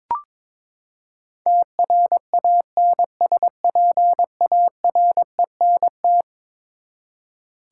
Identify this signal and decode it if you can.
Morse code, heard as a keyed tone, about 22 wpm, 700 Hz. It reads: TRANSPARENT
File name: practice / 22wpm / Eleven Letter Words - 22wpm